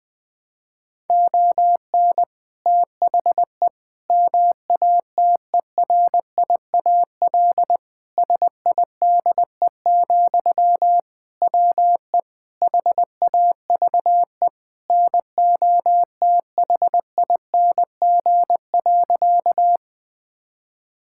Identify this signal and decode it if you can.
Morse code, heard as a keyed tone, about 20 wpm, 700 Hz. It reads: ON THE MATERIAL SIDE, WE HAVE NOTHING.
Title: Sentences from Top 1000 Words - Review Part 5 - 20wpm